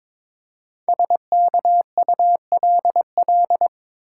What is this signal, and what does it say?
Morse code, heard as a keyed tone, about 22 wpm, 700 Hz. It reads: SKULL
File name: 5 Letter Words - First 3 Letters Given - 22wpm